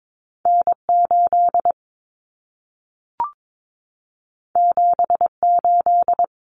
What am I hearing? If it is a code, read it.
Morse code, heard as a keyed tone, about 22 wpm, 700 Hz. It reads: N8 78
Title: Common Cut Numbers - 2-Digit 22wpm